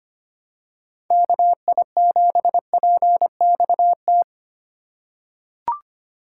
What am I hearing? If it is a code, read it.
Morse code, heard as a keyed tone, about 25 wpm, 700 Hz. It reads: KI7PXT
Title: USA Call Signs - 2x Character Spacing - 25wpm